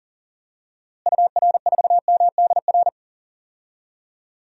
Morse code, heard as a keyed tone, about 37 wpm, 700 Hz. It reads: UR4MDR
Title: DXCC-100 Call Signs - 6 Characters - Repeated 3x - 40wpm